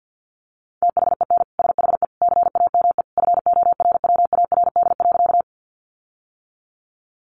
Morse code, heard as a keyed tone, about 50 wpm, 700 Hz. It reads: THEN SHE CAME FORWARD.
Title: Sentences from Top 1300 Words 50wpm